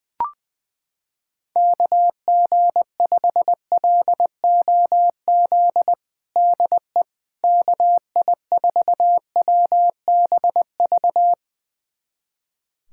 Morse code, heard as a keyed tone, about 20 wpm, 700 Hz. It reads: KG5LOZ DE KI4WBV
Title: USA Callsign DE Callsign - 6 Characters - 20wpm